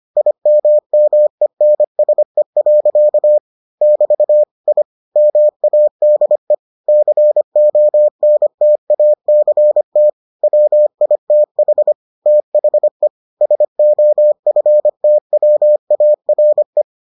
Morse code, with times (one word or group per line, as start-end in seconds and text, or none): 0.2-3.4 IMMENSE.
3.8-4.4 <BT>
4.7-4.8 I
5.2-6.5 MADE
6.9-10.1 CONTACT
10.4-11.9 WITH
12.3-13.1 THE
13.4-16.8 SOFTWARE